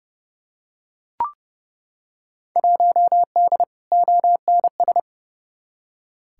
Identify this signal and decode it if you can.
Morse code, heard as a keyed tone, about 30 wpm, 700 Hz. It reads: 1D ONS
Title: Field Day Exchanges - 30wpm